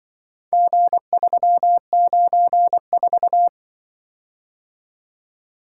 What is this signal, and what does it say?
Morse code, heard as a keyed tone, about 23 wpm, 700 Hz. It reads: G394
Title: Speed-Racing - Four Letter-Number Combinations - 20wpm